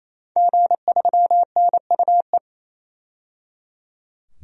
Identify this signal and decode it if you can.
Morse code, heard as a keyed tone, about 28 wpm, 700 Hz. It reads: G3NUE
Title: DXCC-100 Call Signs - 5 Characters - 28wpm